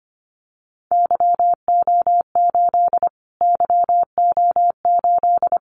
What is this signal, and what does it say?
Morse code, heard as a keyed tone, about 25 wpm, 700 Hz. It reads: YO8 YO8
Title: Callsign Prefixes for Top 25 DXCC - Repeated 3x - 25wpm